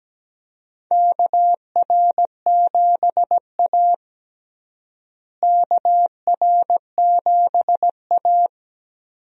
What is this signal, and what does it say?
Morse code, heard as a keyed tone, about 17 wpm, 700 Hz. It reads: KR7A KR7A
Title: USA Call Signs - 4 Characters - Repeated 2x - 17wpm